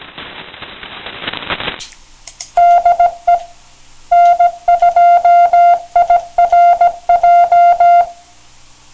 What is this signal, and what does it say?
Morse code, heard as a keyed tone, about 17 wpm, 700 Hz. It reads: DE N2IRJ
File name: TXVerticalRXBeverageSourceN2IRJ